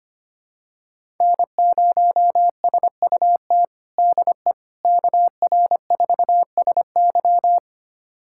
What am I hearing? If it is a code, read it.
Morse code, heard as a keyed tone, about 25 wpm, 700 Hz. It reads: N0SUT DE KR4SY
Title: USA Callsign DE Callsign - 5 Characters - 25wpm